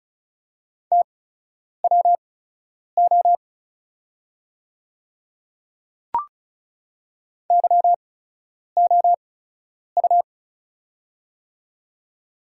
Morse code, heard as a keyed tone, about 34 wpm, spaced as for 9 wpm, 700 Hz. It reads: TWO YOU
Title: 8x Character Spacing - Top 100 Words - 35wpm